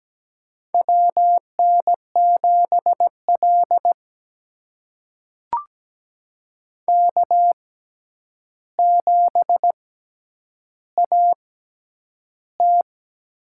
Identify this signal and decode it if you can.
Morse code, heard as a keyed tone, about 17 wpm, spaced as for 9 wpm, 700 Hz. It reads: WN7L K7AT
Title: USA Call Signs - 6x Character Spacing - 17wpm